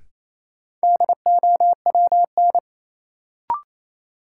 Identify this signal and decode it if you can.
Morse code, heard as a keyed tone, about 28 wpm, 700 Hz. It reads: DOWN
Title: Sentences from Top 300 Words - New Word Drill - 28wpm